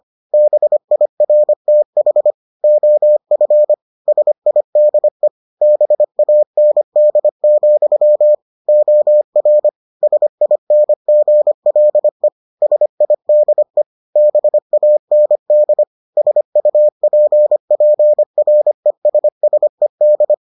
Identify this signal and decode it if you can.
Morse code, heard as a keyed tone, about 25 wpm, 600 Hz. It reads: BIRTH OF SIDE BAND, OR SINGLE SIDE BAND SUPPRESSED